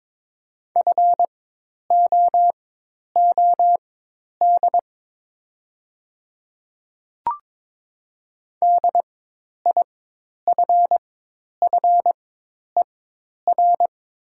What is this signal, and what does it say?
Morse code, heard as a keyed tone, about 22 wpm, spaced as for 10 wpm, 700 Hz. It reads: FOOD DIFFER
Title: 4x Character Spacing - Top 500 Words - 22wpm